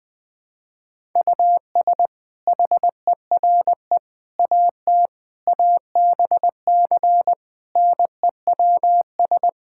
Morse code, heard as a keyed tone, about 20 wpm, 700 Hz. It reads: US HERE AT ABC NEWS
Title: Build A Phrase - 5-Word Phrases - 20wpm